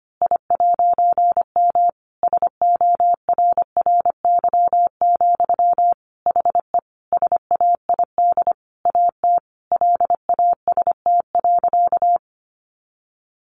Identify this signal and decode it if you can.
Morse code, heard as a keyed tone, about 25 wpm, 700 Hz. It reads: I'M SORRY, HE SAID AT LAST.
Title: A Sound Of Thunder 25wpm